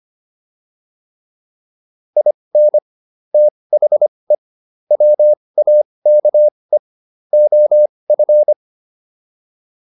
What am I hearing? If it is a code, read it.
Morse code, heard as a keyed tone, about 25 wpm, spaced as for 20 wpm, 600 Hz. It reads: IN THE WAKE OF
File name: WD304 20